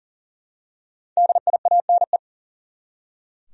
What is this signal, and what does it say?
Morse code, heard as a keyed tone, about 40 wpm, 700 Hz. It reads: DIANE
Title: USA Common Names - Max 5 Letters - 40wpm